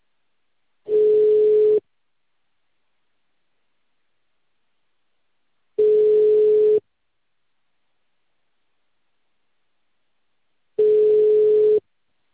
Ringback tone (Europe, 425 Hz).